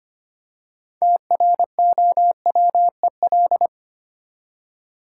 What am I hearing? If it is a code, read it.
Morse code, heard as a keyed tone, about 25 wpm, 700 Hz. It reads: TROWEL
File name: practice / 25wpm / Easy Words with Hints - 25wpm